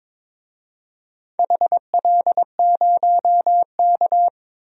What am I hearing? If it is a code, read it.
Morse code, heard as a keyed tone, about 21 wpm, 700 Hz. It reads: HL0K
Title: DXCC-100 Call Signs - 4 Characters - Repeated 3x - 22wpm